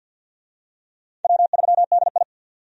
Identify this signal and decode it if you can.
Morse code, heard as a keyed tone, about 52 wpm, 700 Hz. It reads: W3DI
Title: USA Call Signs - 4 Characters - 50wpm